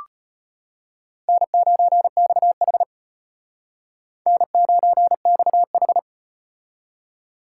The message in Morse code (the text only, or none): N9XH N9XH